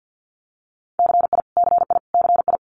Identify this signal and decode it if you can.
Morse code, heard as a keyed tone, about 50 wpm, 700 Hz. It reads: CI CI CI
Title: Multi-letter Phonemes Repeated 3x 50wpm